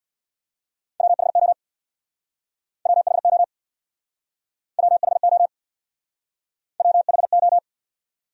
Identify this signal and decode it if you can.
Morse code, heard as a keyed tone, about 60 wpm, 700 Hz. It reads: WHO WHO WHO WHO
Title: Speed-Racing Top 200 Words 50wpm